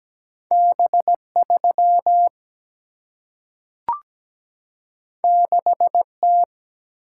Morse code, heard as a keyed tone, about 17 wpm, 700 Hz. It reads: B3 6T
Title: Single-Letter Single-Number 17wpm